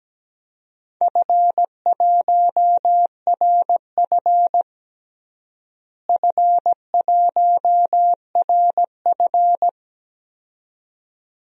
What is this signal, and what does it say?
Morse code, heard as a keyed tone, about 17 wpm, 700 Hz. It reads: F1RF F1RF
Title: DXCC-100 Call Signs - 4 Characters - Repeated 2x - 17wpm